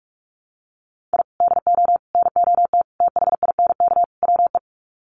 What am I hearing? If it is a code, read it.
Morse code, heard as a keyed tone, about 44 wpm, 700 Hz. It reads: I DO NOT THINK WE
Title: Sets of 5 Words 45wpm